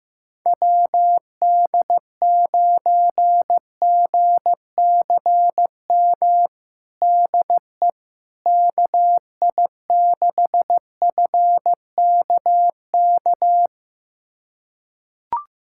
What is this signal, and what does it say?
Morse code, heard as a keyed tone, about 15 wpm, 700 Hz. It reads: WD9GCM DE KI6FKK
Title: USA Callsign DE Callsign - 6 Characters - 15wpm